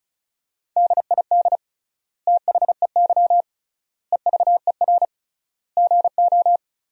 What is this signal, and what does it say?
Morse code, heard as a keyed tone, about 36 wpm, 700 Hz. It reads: DID THEY EVER GO
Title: 3x Word-Spacing - Sentences from Top 400 Words - 35wpm